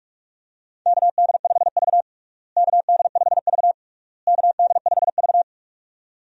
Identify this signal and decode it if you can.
Morse code, heard as a keyed tone, about 46 wpm, 700 Hz. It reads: KD5V KD5V KD5V
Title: USA Call Signs - 4 Characters - Repeated 3x - 45wpm